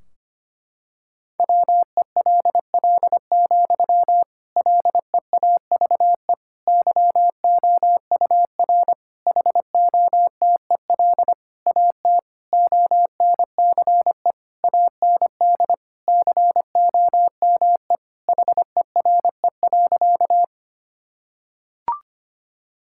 Morse code, heard as a keyed tone, about 25 wpm, 700 Hz. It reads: WELL, LEAVE YOUR HOTEL AT ONCE AND COME HERE.
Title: Sentences from Top 1600 Words 25wpm